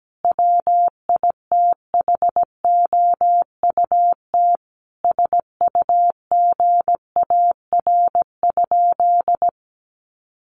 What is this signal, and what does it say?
Morse code, heard as a keyed tone, about 17 wpm, 700 Hz. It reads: WITHOUT SUGAR?
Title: Sentences from Top 1200 Words 17wpm